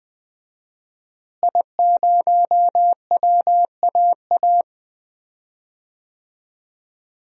Morse code, heard as a keyed tone, about 20 wpm, 700 Hz. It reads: I0WAA
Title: DXCC-100 Call Signs - 5 Characters - Repeated 2x - 20wpm